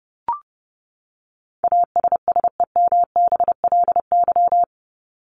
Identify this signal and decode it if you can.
Morse code, heard as a keyed tone, about 30 wpm, 700 Hz. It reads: ASSEMBLY
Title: LY Suffix - Top 100 Words with Base Words 30wpm